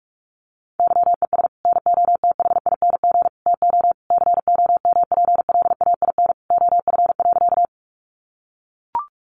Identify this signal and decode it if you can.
Morse code, heard as a keyed tone, about 44 wpm, 700 Hz. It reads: YES NOTHING TO COMPLAIN OF.
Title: Sentences from Top 1300 Words 45wpm